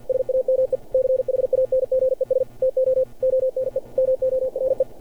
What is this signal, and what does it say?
Morse code, heard as a keyed tone, about 49 wpm, 530 Hz. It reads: SAME CHANGES TO ONE MORE
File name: archived / CRSnet